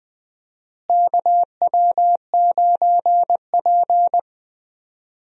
Morse code, heard as a keyed tone, about 20 wpm, 700 Hz. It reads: KW9P
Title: USA Call Signs - 4 Characters - Repeated 3x - 20wpm